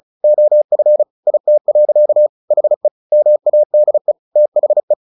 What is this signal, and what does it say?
Morse code, heard as a keyed tone, about 34 wpm, 600 Hz. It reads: OF IT. HE MADE THE
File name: the old man and the sea 2 35wpm